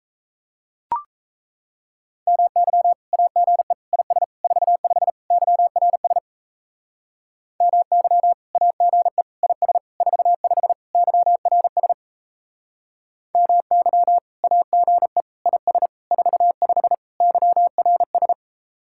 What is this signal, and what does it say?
Morse code, heard as a keyed tone, about 34 wpm, 700 Hz. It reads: MY AGE IS 45 YRS MY AGE IS 45 YRS MY AGE IS 45 YRS
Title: Speed-Racing Common QSO Phrases Encore - 28wpm